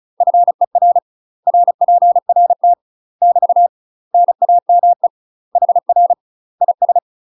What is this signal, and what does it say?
Morse code, heard as a keyed tone, about 35 wpm, 700 Hz. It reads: FER RPRT <BT> NAME HR IS
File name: UK MASTER QSOs TRAINING / 35WPM / UK MASTER QSOs 0004